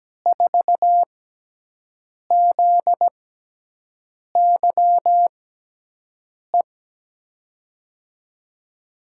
Morse code, heard as a keyed tone, about 17 wpm, spaced as for 6 wpm, 700 Hz. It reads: 4ZYE
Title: USA Call Signs - 6 Characters - 6x Character Spacing - 17wpm